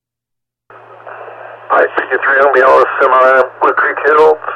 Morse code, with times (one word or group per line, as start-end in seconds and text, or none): none